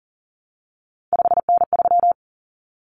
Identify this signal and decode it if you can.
Morse code, heard as a keyed tone, about 39 wpm, 700 Hz. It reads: 5N3
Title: Common Cut Numbers - 3-Digit 40wpm